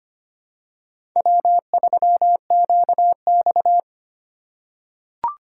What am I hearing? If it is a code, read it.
Morse code, heard as a keyed tone, about 25 wpm, 700 Hz. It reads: W3QX
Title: USA Call Signs - 4x Character Spacing - 25wpm